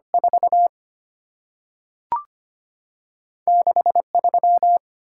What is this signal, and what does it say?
Morse code, heard as a keyed tone, about 25 wpm, 700 Hz. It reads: 4 6 3
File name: Two Letter-Number Combinations - 25wpm